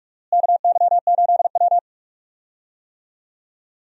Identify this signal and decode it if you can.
Morse code, heard as a keyed tone, about 46 wpm, 700 Hz. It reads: KY8W
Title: USA Call Signs - 4 Characters - Repeated 3x - 45wpm